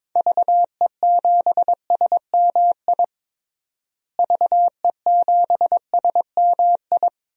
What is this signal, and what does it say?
Morse code, heard as a keyed tone, about 22 wpm, 700 Hz. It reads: VE7SMI VE7SMI
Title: DXCC-100 Call Signs - 6 Characters - Repeated 2x - 22wpm